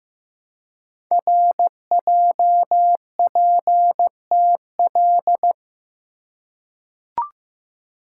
Morse code, heard as a keyed tone, about 15 wpm, 700 Hz. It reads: RJPTL
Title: Code Groups - 5 Letters - 15wpm